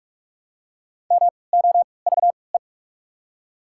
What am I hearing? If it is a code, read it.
Morse code, heard as a keyed tone, about 43 wpm, spaced as for 25 wpm, 690 Hz. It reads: MOVE
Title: 3x Character Spacing - Top 500 Words - 45wpm